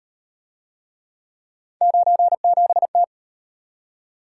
Morse code, heard as a keyed tone, about 31 wpm, 700 Hz. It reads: 97T
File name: Speed-Racing - Three Letter-Number Combinations - 25wpm